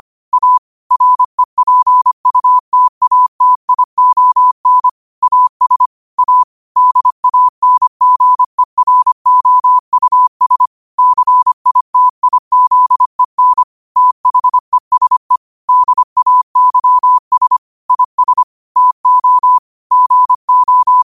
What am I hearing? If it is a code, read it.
Morse code, heard as a keyed tone, about 25 wpm, 1000 Hz. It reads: A REPUTATION AS A DANGEROUS CITIZEN THESE DAYS IS TO GO
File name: qotdfiles / 25wpm